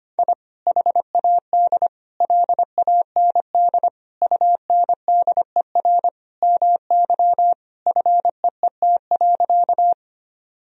Morse code, heard as a keyed tone, about 25 wpm, 700 Hz. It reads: I HAD LAND UNDER MY FEET.
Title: Sentences from Top 1000 Words - Review Part 1 - 25wpm